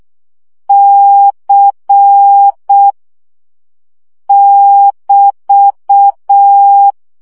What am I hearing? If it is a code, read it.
Morse code, heard as a keyed tone, about 6 wpm, 800 Hz. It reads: C <BT>
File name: CW gyakorlas c 5 baud